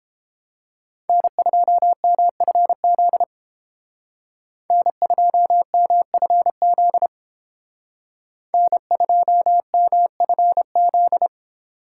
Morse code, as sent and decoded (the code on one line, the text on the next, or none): -. ..--- -- ..-. --.. / -. ..--- -- ..-. --.. / -. ..--- -- ..-. --..
N2MFZ N2MFZ N2MFZ